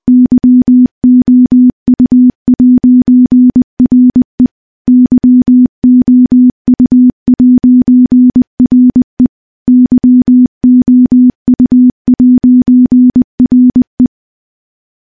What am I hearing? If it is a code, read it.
Morse code, heard as a keyed tone, about 20 wpm, 250 Hz. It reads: YOU'RE YOU'RE YOU'RE
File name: morse / morse-training-20WPM